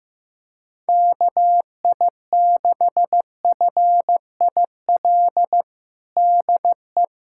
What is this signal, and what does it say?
Morse code, heard as a keyed tone, about 15 wpm, 700 Hz. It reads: KI6FIL DE
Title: USA Callsign DE Callsign 15wpm